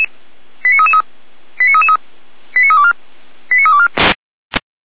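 Five-tone selective call (ZVEI), heard as quick runs of five tones, tones of about 70 ms. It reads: 89333 89324 89324
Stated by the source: ZVEI five-tone call received by SDR.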